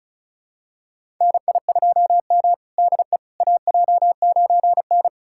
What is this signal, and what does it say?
Morse code, heard as a keyed tone, about 36 wpm, 700 Hz. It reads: NI2M DE AJ9N
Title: USA Callsign DE Callsign - 4 Characters - 35wpm